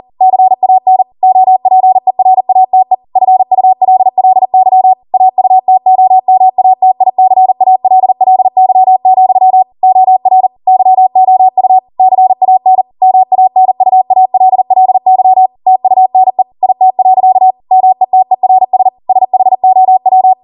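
Morse code, heard as a keyed tone, about 40 wpm, 750 Hz. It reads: CAN OPERATE FULLY AUTOMATICALLY, OR YOU CAN MANUALLY TUNE IT. METERS SHOW
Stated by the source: Code practice at 40 wpm.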